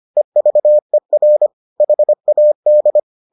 Morse code, heard as a keyed tone, about 25 wpm, 600 Hz. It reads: EVER HAD